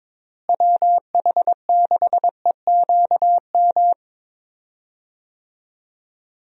Morse code, heard as a keyed tone, about 22 wpm, 700 Hz. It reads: WH6EQM
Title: USA Call Signs - 6 Characters - Repeated 2x - 22wpm